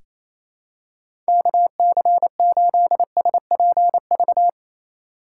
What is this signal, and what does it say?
Morse code, heard as a keyed tone, about 28 wpm, 700 Hz. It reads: KC8SPV
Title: USA Call Signs - 6x Character Spacing - 28wpm